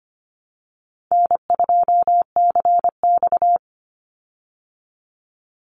Morse code, heard as a keyed tone, about 25 wpm, 700 Hz. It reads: N2CX
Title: Easy Call Signs Repeated 2x - 25wpm